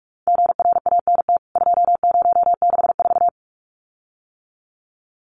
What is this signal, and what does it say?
Morse code, heard as a keyed tone, about 44 wpm, 700 Hz. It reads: GRANT 2064
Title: CWT Exchange 45wpm